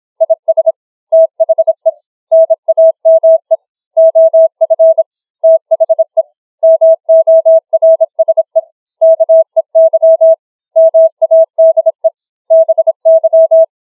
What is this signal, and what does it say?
Morse code, heard as a keyed tone, about 26 wpm, 650 Hz. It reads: IS THE NAME OF THE MORSE KEY MADE BY
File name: challenge-135